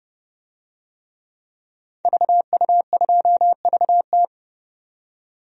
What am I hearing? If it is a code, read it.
Morse code, heard as a keyed tone, about 30 wpm, 700 Hz. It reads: VU2VT
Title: DXCC-100 Call Signs - 5 Characters - 30wpm